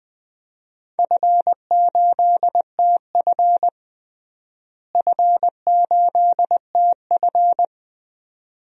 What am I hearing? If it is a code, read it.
Morse code, heard as a keyed tone, about 20 wpm, 700 Hz. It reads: F8TF F8TF
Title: DXCC-100 Call Signs - 4 Characters - Repeated 2x - 20wpm